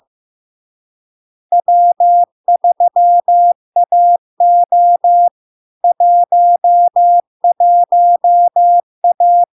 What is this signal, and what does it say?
Morse code, heard as a keyed tone, about 15 wpm, 700 Hz. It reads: W3AO 11A
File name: LARK-FD-2022-Log-in-CW-15wpm-700-Hz